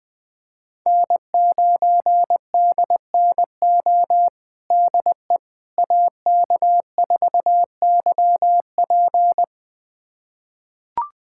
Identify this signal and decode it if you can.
Morse code, heard as a keyed tone, about 20 wpm, 700 Hz. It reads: N9DNO DE AK4YP
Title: USA Callsign DE Callsign - 5 Characters - 20wpm